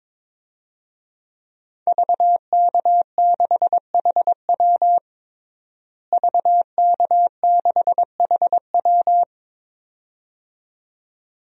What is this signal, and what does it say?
Morse code, heard as a keyed tone, about 22 wpm, 700 Hz. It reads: VK6HW VK6HW
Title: DXCC-100 Call Signs - 5 Characters - Repeated 2x - 22wpm